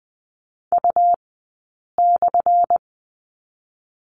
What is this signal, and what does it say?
Morse code, heard as a keyed tone, about 20 wpm, spaced as for 8 wpm, 700 Hz. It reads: U/
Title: 177 Lesson 29d - Slash - Two Characters 20wpm